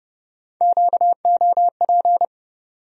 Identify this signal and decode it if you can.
Morse code, heard as a keyed tone, about 30 wpm, 700 Hz. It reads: QOP
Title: USA Call Sign Postfixes - 3 Letters 30wpm